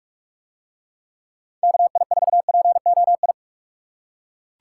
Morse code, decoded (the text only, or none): KI4POI